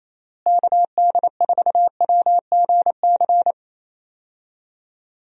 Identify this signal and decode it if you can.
Morse code, heard as a keyed tone, about 27 wpm, 700 Hz. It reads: KD4WGC
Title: USA Call Signs - 6 Characters - Repeated 3x - 28wpm